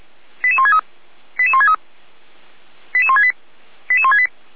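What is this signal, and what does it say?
Five-tone selective call (ZVEI), heard as quick runs of five tones, tones of about 70 ms. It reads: 88163 88163 88168 88168
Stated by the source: Received on an SDR; ZVEI five-tone call 88063.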